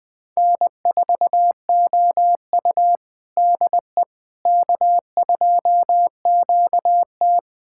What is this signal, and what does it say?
Morse code, heard as a keyed tone, about 19 wpm, 700 Hz. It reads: N4OU DE K2QT
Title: USA Callsign DE Callsign - 4 Characters - 20wpm